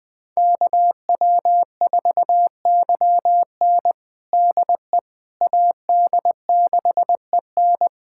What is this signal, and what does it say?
Morse code, heard as a keyed tone, about 21 wpm, 700 Hz. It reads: KW4YN DE AD6EN
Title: USA Callsign DE Callsign - 5 Characters - 20wpm